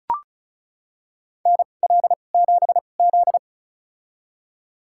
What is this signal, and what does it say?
Morse code, heard as a keyed tone, about 34 wpm, spaced as for 25 wpm, 700 Hz. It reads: NL7Z
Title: USA Call Signs - 2x Character Spacing - 35wpm